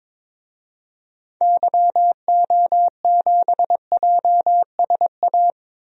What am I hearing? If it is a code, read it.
Morse code, heard as a keyed tone, about 22 wpm, 700 Hz. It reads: YO7JSA
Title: DXCC-100 Call Signs - 6 Characters - 22wpm